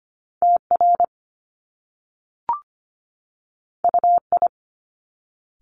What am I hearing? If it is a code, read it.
Morse code, heard as a keyed tone, about 25 wpm, 700 Hz. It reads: TR UI
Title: Top Two Letter Combinations in Words 25wpm